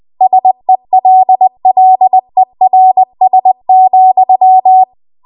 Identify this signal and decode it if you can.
Morse code, heard as a keyed tone, about 20 wpm, 750 Hz. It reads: SELLERS,